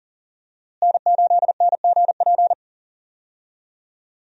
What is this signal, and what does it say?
Morse code, heard as a keyed tone, about 37 wpm, 700 Hz. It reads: N8NGP
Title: USA Call Signs - 5 Characters - Repeated 2x - 40wpm